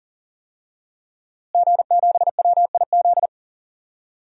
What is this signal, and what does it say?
Morse code, heard as a keyed tone, about 40 wpm, 700 Hz. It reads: G7WIZ
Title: DXCC-100 Call Signs - 5 Characters - Repeated 3x - 40wpm